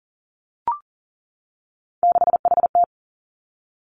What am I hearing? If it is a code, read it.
Morse code, heard as a keyed tone, about 41 wpm, 700 Hz. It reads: 6HT
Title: Three Challenging Letter-Number Combinations 40wpm